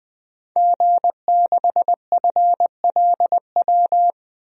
Morse code, heard as a keyed tone, about 20 wpm, 700 Hz. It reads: G6FLW
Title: DXCC-100 Call Signs - 5 Characters - 20wpm